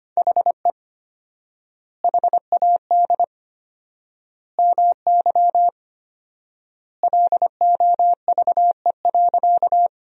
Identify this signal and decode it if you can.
Morse code, heard as a keyed tone, about 25 wpm, 700 Hz. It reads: HE HAD MY LOVE.
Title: 4x Word-Spacing - Sentences from Top 400 Words - 25wpm